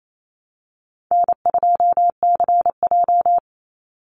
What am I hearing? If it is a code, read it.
Morse code, heard as a keyed tone, about 28 wpm, 700 Hz. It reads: N2CJ
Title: Call Signs - 28wpm